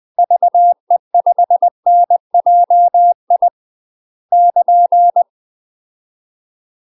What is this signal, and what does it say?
Morse code, heard as a keyed tone, about 20 wpm, 700 Hz. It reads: VE5NJI <KN>
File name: UK MASTER QSOs TRAINING / 20WPM / UK MASTER QSOs 0073